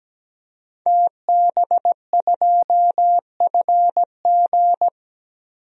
Morse code, heard as a keyed tone, about 17 wpm, 700 Hz. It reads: TB2FG
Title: DXCC-100 Call Signs - 5 Characters - Repeated 3x - 17wpm